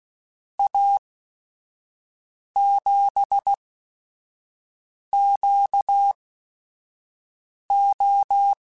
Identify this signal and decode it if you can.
Morse code, heard as a keyed tone, about 16 wpm, spaced as for 5 wpm, 800 Hz. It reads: A7QO
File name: QSOs / QSOs-mp3s / WA7QOX 1@5